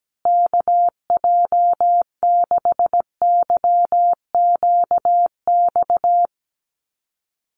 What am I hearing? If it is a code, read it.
Morse code, heard as a keyed tone, about 17 wpm, 700 Hz. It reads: KJ6YQX
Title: USA Call Signs Repeated 3x 17wpm